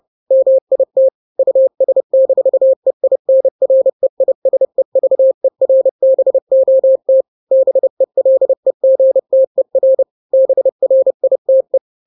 Morse code, heard as a keyed tone, about 30 wpm, 530 Hz. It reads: MIT US-EINREISEVERBOT BELEGTER BRITE
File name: daily / 29wpm